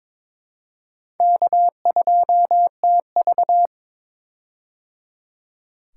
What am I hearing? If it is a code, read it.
Morse code, heard as a keyed tone, about 22 wpm, 700 Hz. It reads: K2TV
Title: K1USN SST QSO Practice - 22wpm